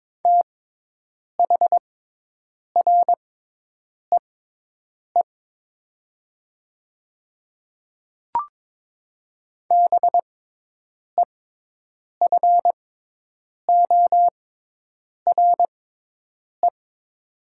Morse code, heard as a keyed tone, about 22 wpm, spaced as for 8 wpm, 700 Hz. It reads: THREE BEFORE
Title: 6x Character Spacing - Top 200 Words - 22wpm